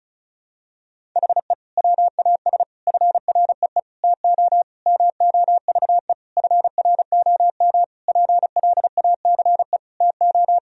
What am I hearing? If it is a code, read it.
Morse code, heard as a keyed tone, about 36 wpm, 700 Hz. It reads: HE WAS FREE TO MOVE FROM PLACE TO
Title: Sentences from Top 1000 Words - Review Part 1 - 35wpm